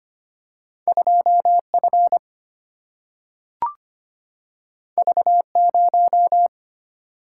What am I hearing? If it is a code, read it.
Morse code, heard as a keyed tone, about 25 wpm, 700 Hz. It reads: 2F V0
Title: Single-Letter Single-Number 25wpm